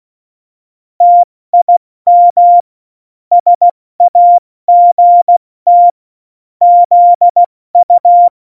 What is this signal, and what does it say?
Morse code, heard as a keyed tone, about 16 wpm, spaced as for 14 wpm, 700 Hz. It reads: TIM SAGT ZU
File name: Morsekurs / Witze / Witz 21. August 2018, 70 F80